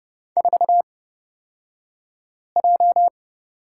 Morse code, heard as a keyed tone, about 30 wpm, spaced as for 5 wpm, 700 Hz. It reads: 4J
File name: Lesson 36d - BT - Two Characters 30wpm